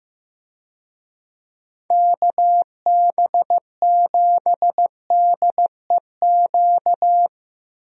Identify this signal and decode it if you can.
Morse code, heard as a keyed tone, about 15 wpm, 700 Hz. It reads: KB7DEQ
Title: USA Call Signs - 6 Characters - Repeated 2x - 15wpm